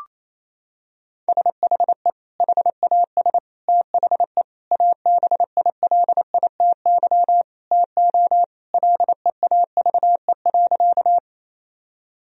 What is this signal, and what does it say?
Morse code, heard as a keyed tone, about 28 wpm, 700 Hz. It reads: SHE HAS THE ABILITY TO LEAVE.
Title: Sentences from Top 1000 Words - Review Part 3 - 28wpm